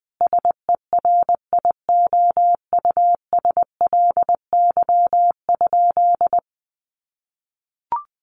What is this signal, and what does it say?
Morse code, heard as a keyed tone, about 20 wpm, 700 Hz. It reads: SERIOUSLY?
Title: Sentences from Top 1200 Words 20wpm